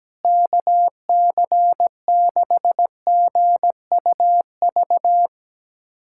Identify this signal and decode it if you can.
Morse code, heard as a keyed tone, about 17 wpm, 700 Hz. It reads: KC6GUV
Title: USA Callsign DE Callsign 17wpm